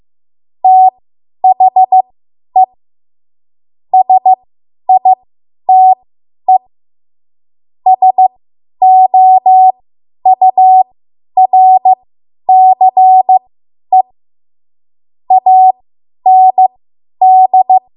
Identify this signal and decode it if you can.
Morse code, heard as a keyed tone, about 15 wpm, spaced as for 10 wpm, 750 Hz. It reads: THE SITE SOURCE AND
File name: Morse / 10WPM